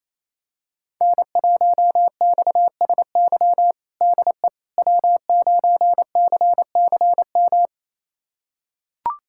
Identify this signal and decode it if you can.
Morse code, heard as a keyed tone, about 28 wpm, 700 Hz. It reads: N1XSY DE W9CCM
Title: USA Callsign DE Callsign - 5 Characters - 28wpm